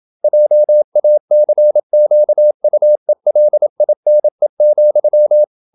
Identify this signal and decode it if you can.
Morse code, heard as a keyed tone, about 27 wpm, 600 Hz. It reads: JACQUELINE,
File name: Never 0027wpm